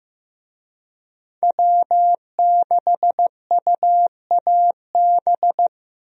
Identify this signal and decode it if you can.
Morse code, heard as a keyed tone, about 15 wpm, 700 Hz. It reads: W6UAB
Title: USA Call Signs - 5 Characters - Repeated 2x - 15wpm